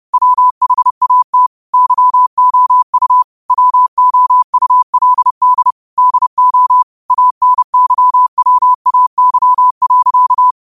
Morse code, heard as a keyed tone, about 30 wpm, 1000 Hz. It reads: WHAT YOU WOULD DO ANYWAY.
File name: qotdfiles / 30wpm